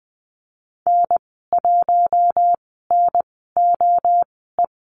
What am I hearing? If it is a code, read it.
Morse code, heard as a keyed tone, about 20 wpm, spaced as for 15 wpm, 700 Hz. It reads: N1NOE
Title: 042 Lesson 8g - Number 1 - Call Signs 20wpm